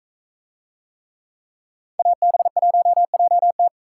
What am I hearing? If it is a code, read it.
Morse code, heard as a keyed tone, about 41 wpm, 700 Hz. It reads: AB1JT